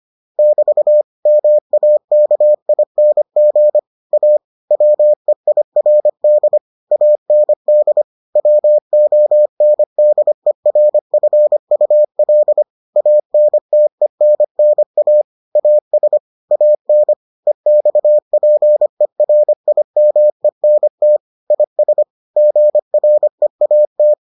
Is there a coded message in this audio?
Yes: Morse code from 0.4 to 24.1 s, heard as a keyed tone, about 25 wpm, 600 Hz. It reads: <BT> MAKING A WEIRD AND WONDERFUL ANTENNA AS AN EXPERIMENT IS GREAT